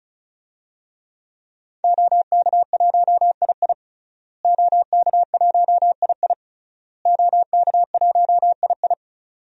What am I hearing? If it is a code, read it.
Morse code, heard as a keyed tone, about 36 wpm, 700 Hz. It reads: OK1II OK1II OK1II
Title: DXCC-100 Call Signs - 5 Characters - Repeated 3x - 35wpm